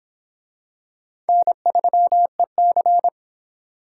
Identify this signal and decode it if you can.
Morse code, heard as a keyed tone, about 23 wpm, 700 Hz. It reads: N3EC